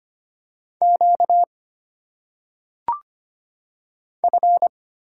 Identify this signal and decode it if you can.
Morse code, heard as a keyed tone, about 25 wpm, 700 Hz. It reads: Q F
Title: Single Letters - Phonetic Pronunciation 25wpm